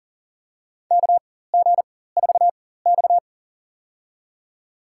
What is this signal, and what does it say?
Morse code, heard as a keyed tone, about 40 wpm, spaced as for 19 wpm, 700 Hz. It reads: KG4X
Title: USA Call Signs - 4 Characters - 4x Character Spacing - 40wpm